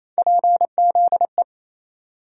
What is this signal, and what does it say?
Morse code, heard as a keyed tone, about 27 wpm, 700 Hz. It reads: PZE
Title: USA Call Sign Postfixes - 3 Letters 28wpm